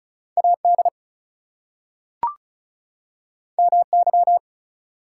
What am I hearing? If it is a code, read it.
Morse code, heard as a keyed tone, about 36 wpm, 700 Hz. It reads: AD MY
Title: Two Letter Words - 35wpm